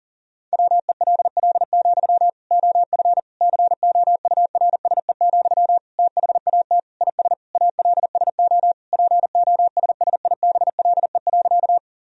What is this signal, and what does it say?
Morse code, heard as a keyed tone, about 37 wpm, 700 Hz. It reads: WELL, OF COURSE, THAT IS ALSO POSSIBLE.
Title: Sentences from Top 1000 Words - Review Part 1 - 40wpm